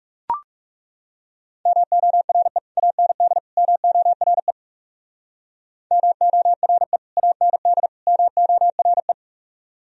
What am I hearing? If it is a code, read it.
Morse code, heard as a keyed tone, about 37 wpm, 700 Hz. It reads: MORE AND MORE MORE AND MORE
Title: Speed-Racing - Sets of 3 Words 30wpm